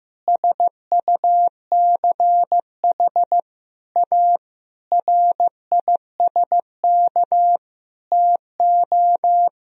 Morse code, heard as a keyed tone, about 15 wpm, 700 Hz. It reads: SUCH A RISK TO
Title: Sentences from Top 1000 Words - Review Part 4 - 15wpm